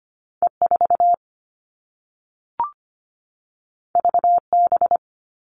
Morse code, heard as a keyed tone, about 25 wpm, 700 Hz. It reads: E 4 V B
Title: All Cut Numbers - 2-Digit 25wpm